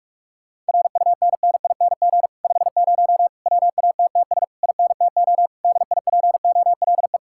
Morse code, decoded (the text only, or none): RUNNING 50 WATTS INTO DIPOLE